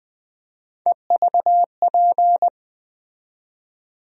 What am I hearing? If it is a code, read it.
Morse code, heard as a keyed tone, about 20 wpm, 700 Hz. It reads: EVP
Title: USA Call Sign Postfixes - 3 Letters 20wpm